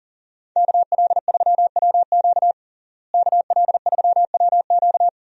Morse code, heard as a keyed tone, about 37 wpm, 700 Hz. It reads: KL3WQ KL3WQ
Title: USA Call Signs - 5 Characters - Repeated 3x - 40wpm